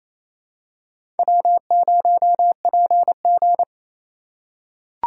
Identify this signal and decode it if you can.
Morse code, heard as a keyed tone, about 28 wpm, 700 Hz. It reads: W0PG
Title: USA Call Signs - 4 Characters - 3x Character Spacing - 28wpm